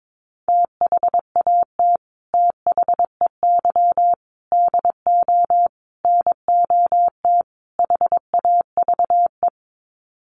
Morse code, heard as a keyed tone, about 22 wpm, 700 Hz. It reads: THAT THEY DO NOT HAVE
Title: Sets of 5 Words Encore2 22wpm